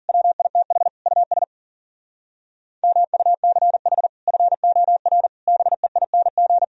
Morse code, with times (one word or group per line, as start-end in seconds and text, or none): 0.1-0.9 WITH
1.1-1.4 US
2.8-4.1 MUCH
4.3-5.3 FOR
5.5-6.6 BEING